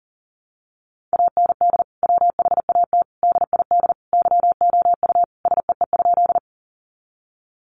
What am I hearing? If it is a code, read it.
Morse code, heard as a keyed tone, about 40 wpm, 700 Hz. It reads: AND WHAT DID YOU SEE?
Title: Sentences from Top 100 Words 40wpm